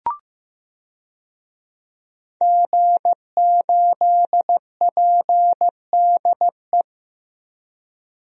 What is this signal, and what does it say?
Morse code, heard as a keyed tone, about 15 wpm, 700 Hz. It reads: G8PDE
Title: DXCC-100 Call Signs - 5 Characters - 15wpm